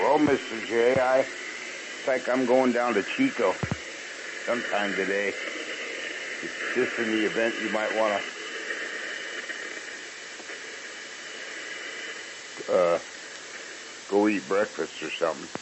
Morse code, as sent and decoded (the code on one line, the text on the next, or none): none